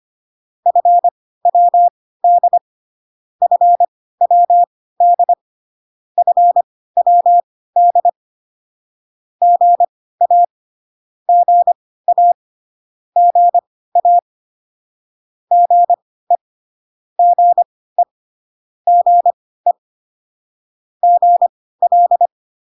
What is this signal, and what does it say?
Morse code, heard as a keyed tone, about 25 wpm, spaced as for 16 wpm, 700 Hz. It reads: FWD FWD FWD GA GA GA GE GE GE GL